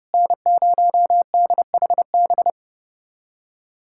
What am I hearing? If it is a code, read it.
Morse code, heard as a keyed tone, about 30 wpm, 700 Hz. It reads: N0DHB
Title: USA Callsign DE Callsign - 5 Characters - 30wpm